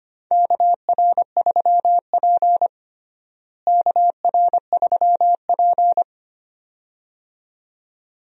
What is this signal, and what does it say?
Morse code, heard as a keyed tone, about 25 wpm, 700 Hz. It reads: KR3P KR3P
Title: USA Call Signs - 4 Characters - Repeated 2x - 25wpm